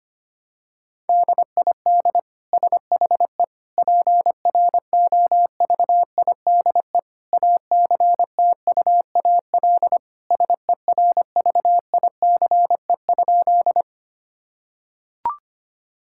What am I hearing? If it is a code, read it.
Morse code, heard as a keyed tone, about 25 wpm, 700 Hz. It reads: DID SHE PROVIDE ACTUAL SERVICE?
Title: Sentences from Top 1000 Words - Review Part 5 - 25wpm